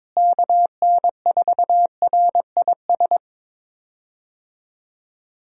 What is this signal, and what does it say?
Morse code, heard as a keyed tone, about 22 wpm, 700 Hz. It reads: KN4RIS
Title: USA Call Signs - 6 Characters - Repeated 3x - 22wpm